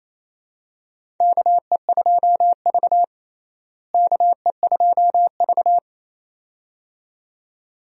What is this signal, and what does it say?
Morse code, heard as a keyed tone, about 28 wpm, 700 Hz. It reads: KE2V KE2V
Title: USA Call Signs - 4 Characters - Repeated 2x - 28wpm